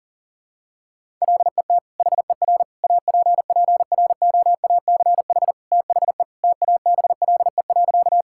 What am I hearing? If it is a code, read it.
Morse code, heard as a keyed tone, about 40 wpm, 700 Hz. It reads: LET HER APPROACH THE TABLE.
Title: Sentences from Top 1000 Words - Review Part 6 - 40wpm